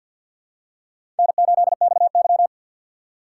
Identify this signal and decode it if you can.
Morse code, heard as a keyed tone, about 46 wpm, 700 Hz. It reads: N8XY
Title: USA Call Signs - 4 Characters - 3x Character Spacing - 50wpm